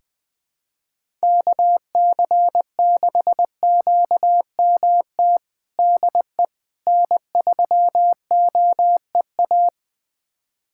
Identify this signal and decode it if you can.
Morse code, heard as a keyed tone, about 20 wpm, 700 Hz. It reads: KC6QMT DE N3OEA
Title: USA Callsign DE Callsign 20wpm